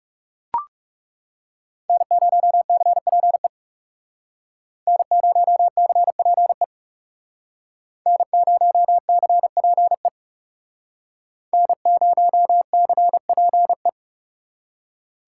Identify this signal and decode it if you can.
Morse code, heard as a keyed tone, about 39 wpm, 700 Hz. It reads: N0CPE N0CPE N0CPE N0CPE